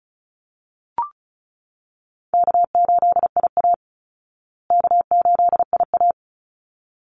Morse code, heard as a keyed tone, about 34 wpm, 700 Hz. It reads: K8IA K8IA
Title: Easy Call Signs Repeated 2x - 35wpm